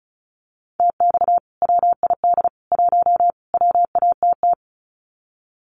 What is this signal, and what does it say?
Morse code, heard as a keyed tone, about 34 wpm, 700 Hz. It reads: TX WID 1 WATT
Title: Common QSO Phrases 35wpm